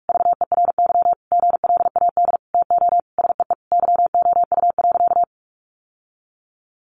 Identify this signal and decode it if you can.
Morse code, heard as a keyed tone, about 44 wpm, 700 Hz. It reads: VERY GLAD TO SEE YOU.
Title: Sentences from Top 800 Words 45wpm